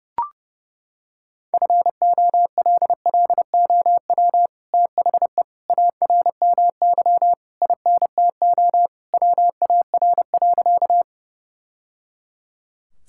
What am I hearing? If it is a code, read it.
Morse code, heard as a keyed tone, about 30 wpm, 700 Hz. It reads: FOLLOW THE ARMY INTO WAR.
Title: Sentences from Top 1000 Words - Review Part 5 - 30wpm